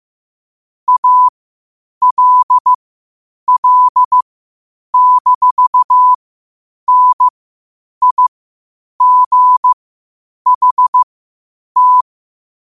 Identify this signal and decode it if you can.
Morse code, heard as a keyed tone, about 15 wpm, spaced as for 8 wpm, 1000 Hz. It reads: ALL-NIGHT
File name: qotdfiles / 8wpm